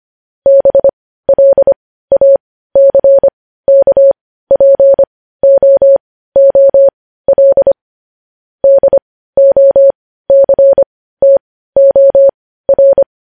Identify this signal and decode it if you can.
Morse code, heard as a keyed tone, about 25 wpm, spaced as for 15 wpm, 550 Hz. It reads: BLACKPOOL DOCTOR
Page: News Headlines in Morse Code at 15 WPM